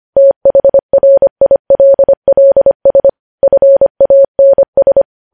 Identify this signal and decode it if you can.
Morse code, heard as a keyed tone, about 25 wpm, 550 Hz. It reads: THRILLS FANS
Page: News Headlines in Morse Code at 25 WPM